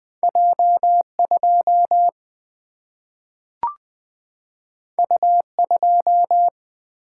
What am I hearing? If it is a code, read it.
Morse code, heard as a keyed tone, about 20 wpm, 700 Hz. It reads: J 2 U 2
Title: Single-Letter Single-Number 20wpm